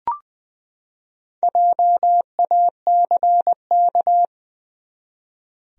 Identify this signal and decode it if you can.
Morse code, heard as a keyed tone, about 20 wpm, 700 Hz. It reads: JACK
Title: USA Common Names - Max 5 Letters - 20wpm